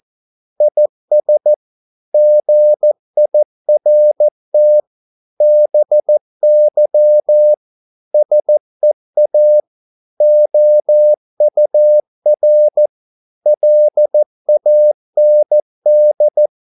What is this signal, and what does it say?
Morse code, heard as a keyed tone, about 14 wpm, 600 Hz. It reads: IS GIRT BY SEA OUR LAND